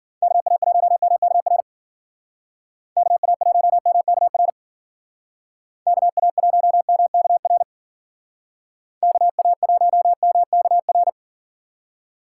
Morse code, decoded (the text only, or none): KA1MKR KA1MKR KA1MKR KA1MKR